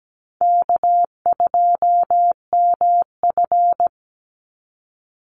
Morse code, heard as a keyed tone, about 17 wpm, 700 Hz. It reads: K2MF
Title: Easy Call Signs Repeated 2x - 17wpm